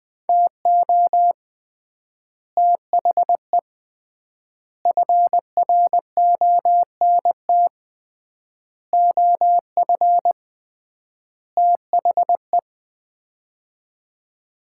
Morse code, heard as a keyed tone, about 21 wpm, 700 Hz. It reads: TO THE FRONT OF THE
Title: 3x Word-Spacing - Sets of 5 Words - 20wpm